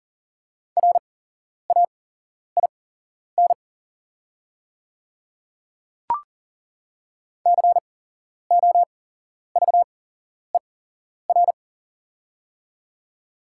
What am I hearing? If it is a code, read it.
Morse code, heard as a keyed tone, about 40 wpm, spaced as for 11 wpm, 700 Hz. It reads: RAIN COVER
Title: 8x Character Spacing - Top 300 Words - 40wpm